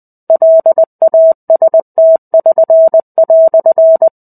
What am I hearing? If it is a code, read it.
Morse code, heard as a keyed tone, about 20 wpm, 650 Hz. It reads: LAST<SN>"
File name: 05-Chapter-05 20 WPM